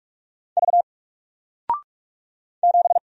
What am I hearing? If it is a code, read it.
Morse code, heard as a keyed tone, about 44 wpm, 700 Hz. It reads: V 7
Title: Single Character (Letter:Number:Punctuation:Prosign) 45wpm